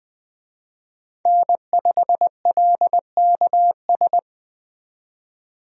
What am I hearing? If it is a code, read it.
Morse code, heard as a keyed tone, about 20 wpm, 700 Hz. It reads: N5LKS
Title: USA Call Signs - 5 Characters - 20wpm